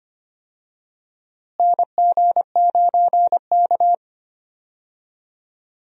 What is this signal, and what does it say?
Morse code, heard as a keyed tone, about 25 wpm, 700 Hz. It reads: NG9K
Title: USA Call Signs - 4 Characters - Repeated 2x - 25wpm